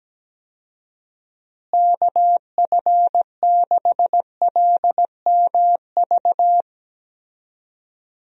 Morse code, heard as a keyed tone, about 17 wpm, 700 Hz. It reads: KF6LMV
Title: USA Call Signs - 6 Characters - Repeated 3x - 17wpm